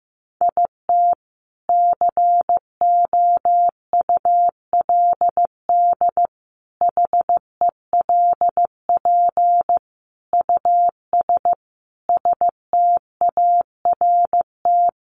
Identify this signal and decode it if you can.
Morse code, heard as a keyed tone, about 15 wpm, 700 Hz. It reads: IT COULD HELP US START
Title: Sentences from Top 1400 Words 15wpm